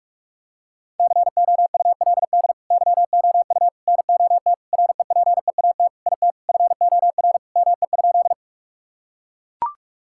Morse code, heard as a keyed tone, about 41 wpm, 700 Hz. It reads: COULD YOU NOT REPEAT IT FOR ME?
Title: Sentences from Top 1600 Words 45wpm